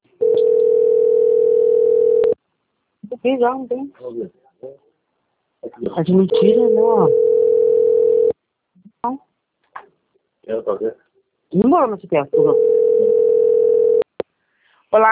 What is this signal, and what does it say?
Ringback tone (North America), 440 + 480 Hz, with speech.
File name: out- - - - - .84312